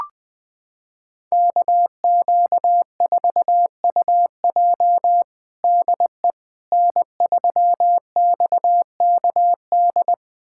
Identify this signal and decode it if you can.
Morse code, heard as a keyed tone, about 20 wpm, 700 Hz. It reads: KQ4UJ DE N3XKD
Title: USA Callsign DE Callsign - 5 Characters - 20wpm